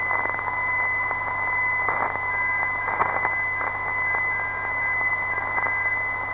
Morse code, heard as a keyed tone, about 10 wpm, 1600 Hz, with noise. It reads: ZRA
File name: ndb zvok / PDA-332kHz dt